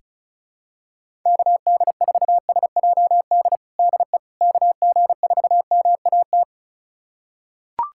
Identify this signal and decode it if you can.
Morse code, heard as a keyed tone, about 36 wpm, 700 Hz. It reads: KD4SJD DE KG4MAT
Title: USA Callsign DE Callsign 35wpm